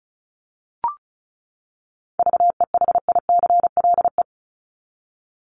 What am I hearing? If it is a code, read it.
Morse code, heard as a keyed tone, about 34 wpm, 700 Hz. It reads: VEHICLE
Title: Top 1751-2000 Words 35wpm